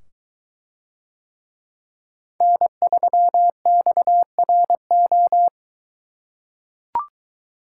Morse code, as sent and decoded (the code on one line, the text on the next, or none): -. ...-- -..- .-. ---
N3XRO